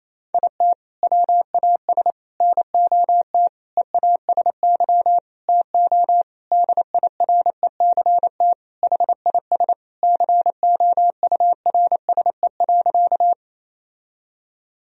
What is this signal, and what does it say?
Morse code, heard as a keyed tone, about 27 wpm, 700 Hz. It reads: IT WAS NOT EASY TO DIRECT HIS COURSE.
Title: Sentences from Top 1000 Words - Review Part 2 - 28wpm